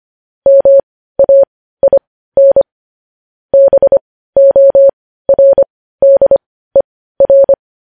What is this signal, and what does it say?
Morse code, heard as a keyed tone, about 25 wpm, spaced as for 15 wpm, 550 Hz. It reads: MAIN BORDER
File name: podcast / News headlines-1x-25x15